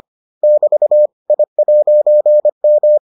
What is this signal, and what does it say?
Morse code, heard as a keyed tone, about 25 wpm, 600 Hz. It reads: <BT> I'M